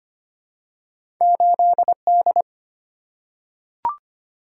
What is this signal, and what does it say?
Morse code, heard as a keyed tone, about 25 wpm, 690 Hz. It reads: 8D
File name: Two Letter-Number Combinations - 25wpm